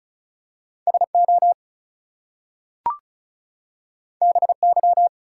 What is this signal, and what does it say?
Morse code, heard as a keyed tone, about 36 wpm, 700 Hz. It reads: SO BY
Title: Two Letter Words - 35wpm